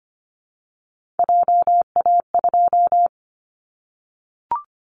Morse code, heard as a keyed tone, about 25 wpm, 700 Hz. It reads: JA2
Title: Callsign Prefixes for Top 50 DXCC - 25wpm